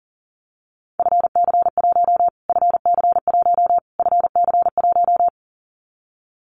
Morse code, heard as a keyed tone, about 40 wpm, 700 Hz. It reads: FC1 FC1 FC1
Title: Callsign Prefixes for Top 25 DXCC - Repeated 3x - 40wpm